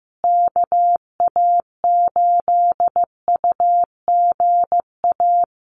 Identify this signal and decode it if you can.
Morse code, heard as a keyed tone, about 15 wpm, 700 Hz. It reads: KA8UGA
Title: USA Call Signs Repeated 2x 15wpm